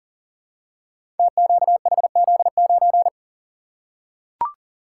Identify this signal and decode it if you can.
Morse code, heard as a keyed tone, about 42 wpm, 690 Hz. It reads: TQHZ9
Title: Code Groups - 5 Letter:Numbers - 40wpm